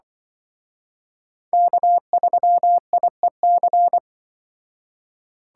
Morse code, heard as a keyed tone, about 23 wpm, 700 Hz. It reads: K3IEC
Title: Speed Racing - USA 1x3 Call Signs 20wpm